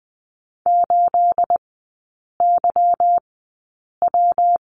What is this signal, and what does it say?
Morse code, heard as a keyed tone, about 20 wpm, spaced as for 8 wpm, 690 Hz. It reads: 8YW
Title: 212 Lesson 34e - Number 8 - Three Characters 20wpm